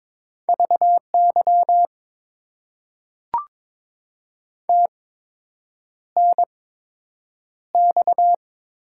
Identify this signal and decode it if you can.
Morse code, heard as a keyed tone, about 22 wpm, spaced as for 8 wpm, 700 Hz. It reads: VY TNX
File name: Top 90 QSO Elements - 8x Character Spacing - 22wpm